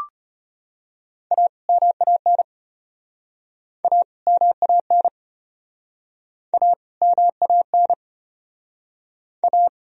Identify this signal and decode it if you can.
Morse code, heard as a keyed tone, about 34 wpm, 700 Hz. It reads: A MAN A MAN A MAN A